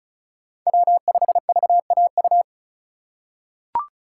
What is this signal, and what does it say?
Morse code, heard as a keyed tone, about 36 wpm, 700 Hz. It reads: W5VAU
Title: USA Call Signs - 2x Character Spacing - 35wpm